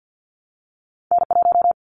Morse code, heard as a keyed tone, about 50 wpm, 700 Hz. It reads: N1
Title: Callsign Prefixes for Top 25 DXCC - Repeated 3x - 50wpm